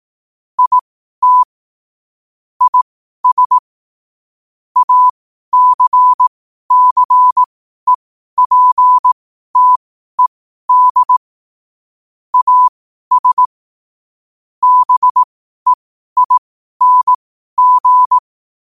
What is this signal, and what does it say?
Morse code, heard as a keyed tone, about 18 wpm, spaced as for 12 wpm, 1000 Hz. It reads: IT IS ACCEPTED AS BEING
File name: qotdfiles / 12wpm